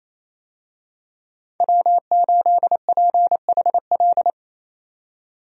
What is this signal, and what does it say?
Morse code, heard as a keyed tone, about 28 wpm, 700 Hz. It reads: W8PHL
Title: USA Call Signs - 5 Characters - 28wpm